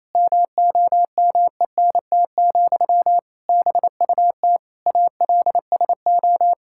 Morse code, heard as a keyed tone, about 28 wpm, 700 Hz. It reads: MOMENT, BUT ALSO
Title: Sentences from Top 1000 Words - Review Part 6 - 28wpm